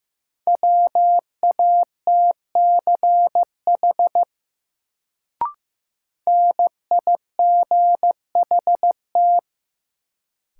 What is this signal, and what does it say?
Morse code, heard as a keyed tone, about 15 wpm, 700 Hz. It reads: WATCH NIGHT
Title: Sentences from Top 300 Words - New Word Drill - 15wpm